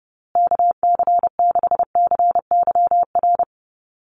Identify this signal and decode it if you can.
Morse code, heard as a keyed tone, about 30 wpm, 700 Hz. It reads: KC6CYR
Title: USA Call Signs Repeated 2x 30wpm